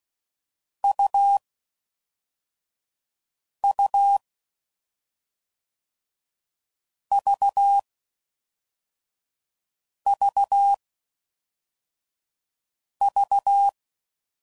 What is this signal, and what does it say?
Morse code, heard as a keyed tone, about 16 wpm, 800 Hz. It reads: U U V V V
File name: Combinations / Difficult Combos / AUV AUV4